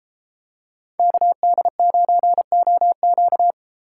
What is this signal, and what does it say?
Morse code, heard as a keyed tone, about 32 wpm, 700 Hz. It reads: KD9OQ